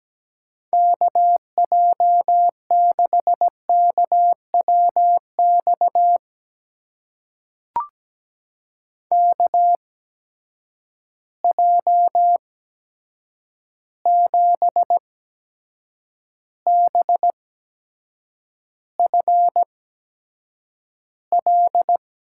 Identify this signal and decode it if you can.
Morse code, heard as a keyed tone, about 17 wpm, spaced as for 7 wpm, 700 Hz. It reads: KJ6KWX KJ7BFL
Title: USA Call Signs - 6 Characters - 8x Character Spacing - 17wpm